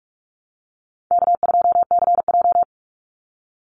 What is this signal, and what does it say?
Morse code, heard as a keyed tone, about 44 wpm, 700 Hz. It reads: K2CJ
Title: Easy Call Signs Repeated 3x - 45wpm